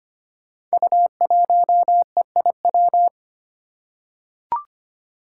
Morse code, heard as a keyed tone, about 25 wpm, 700 Hz. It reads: U1EIW
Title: Five Challenging Letter-Number Combinations 25wpm